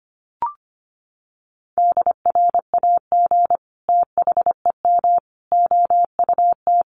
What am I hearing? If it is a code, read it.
Morse code, heard as a keyed tone, about 25 wpm, 700 Hz. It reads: DRAG THEM OUT
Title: Sentences from Top 1500 Words 25wpm